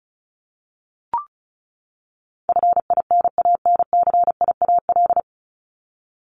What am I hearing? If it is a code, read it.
Morse code, heard as a keyed tone, about 34 wpm, 700 Hz. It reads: FINANCIAL
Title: Top 751-1000 Words 35wpm